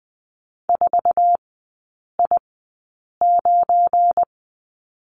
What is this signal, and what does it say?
Morse code, heard as a keyed tone, about 20 wpm, spaced as for 8 wpm, 700 Hz. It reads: 4I9
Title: 172 Lesson 28e - Number 9 - Three Characters 20wpm